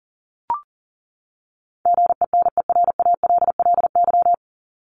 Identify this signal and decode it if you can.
Morse code, heard as a keyed tone, about 40 wpm, 700 Hz. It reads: GENERALLY
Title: Top 501-750 Words 40wpm